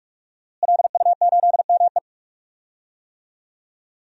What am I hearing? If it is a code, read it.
Morse code, heard as a keyed tone, about 41 wpm, 700 Hz. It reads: LU8ME
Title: DXCC-100 Call Signs - 5 Characters - Repeated 2x - 45wpm